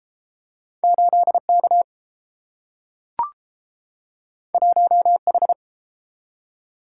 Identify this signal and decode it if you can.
Morse code, heard as a keyed tone, about 30 wpm, 700 Hz. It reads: 8K 1H